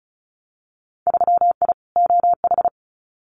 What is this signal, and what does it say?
Morse code, heard as a keyed tone, about 36 wpm, 700 Hz. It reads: 3I OH
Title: Winter Field Day Exchanges Encore 35wpm